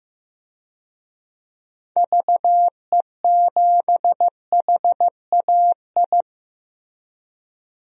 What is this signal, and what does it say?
Morse code, heard as a keyed tone, about 16 wpm, 700 Hz. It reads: VE7HAI
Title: DXCC-100 Call Signs - 6 Characters - Repeated 2x - 15wpm